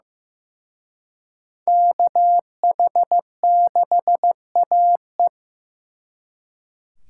Morse code, heard as a keyed tone, about 18 wpm, 700 Hz. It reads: KH6AE
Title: Speed Racing - USA 2x2 Call Signs 15wpm